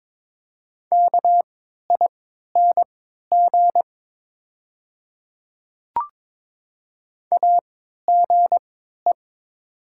Morse code, heard as a keyed tone, about 22 wpm, spaced as for 12 wpm, 700 Hz. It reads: KING AGE
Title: 3x Character Spacing - Top 500 Words - 22wpm